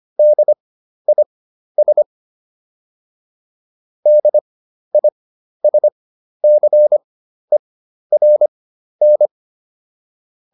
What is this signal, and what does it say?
Morse code, heard as a keyed tone, about 25 wpm, spaced as for 12 wpm, 600 Hz. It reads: DIS DISCERN